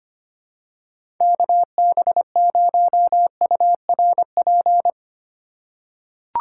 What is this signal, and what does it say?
Morse code, heard as a keyed tone, about 25 wpm, 700 Hz. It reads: KB0URP
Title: USA Call Signs - 6 Characters - 4x Character Spacing - 25wpm